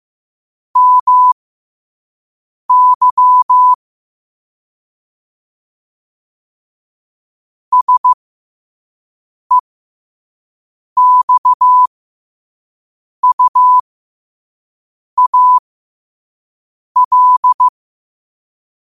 Morse code, heard as a keyed tone, about 15 wpm, spaced as for 5 wpm, 1000 Hz. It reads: MY SEXUAL